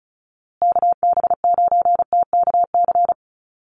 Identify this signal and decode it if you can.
Morse code, heard as a keyed tone, about 34 wpm, 700 Hz. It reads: KB9TKC
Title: USA Call Signs Repeated 2x 35wpm